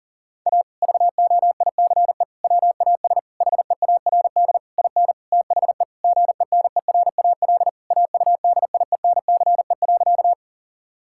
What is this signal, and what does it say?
Morse code, heard as a keyed tone, about 40 wpm, 700 Hz. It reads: A VOICE WAS HEARD IN THE GENERAL AUDIENCE.
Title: Sentences from Top 1000 Words - Review Part 5 - 40wpm